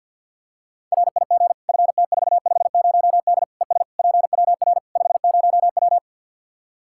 Morse code, heard as a keyed tone, about 52 wpm, 700 Hz. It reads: RIG FT450D ES PWR 50W
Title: K1USN SST QSO Practice 50wpm